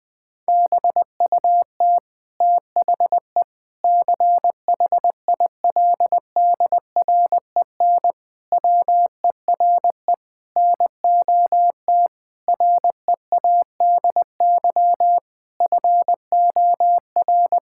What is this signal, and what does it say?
Morse code, heard as a keyed tone, about 20 wpm, 700 Hz. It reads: BUT THE CHILDREN WERE NOT READY FOR
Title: Sentences from Top 1000 Words - Review Part 6 - 20wpm